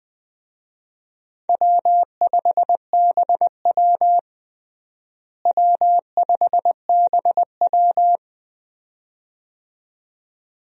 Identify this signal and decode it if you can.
Morse code, heard as a keyed tone, about 21 wpm, 700 Hz. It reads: W5BW W5BW
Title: USA Call Signs - 4 Characters - Repeated 2x - 20wpm